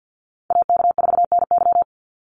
Morse code, heard as a keyed tone, about 50 wpm, 700 Hz. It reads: AK4NY
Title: Call Signs - 50wpm